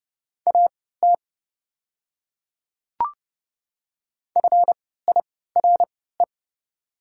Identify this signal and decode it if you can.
Morse code, heard as a keyed tone, about 29 wpm, spaced as for 17 wpm, 700 Hz. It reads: AT FIRE
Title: 3x Character Spacing - Top 500 Words - 30wpm